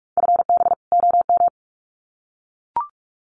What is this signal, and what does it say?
Morse code, heard as a keyed tone, about 44 wpm, 700 Hz. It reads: FB OM
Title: Common QSO Phrases 45wpm